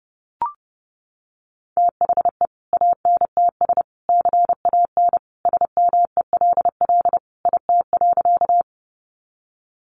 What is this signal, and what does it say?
Morse code, heard as a keyed tone, about 30 wpm, 700 Hz. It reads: THE ANTS CAN SMELL IT.
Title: Easy Phrases 30wpm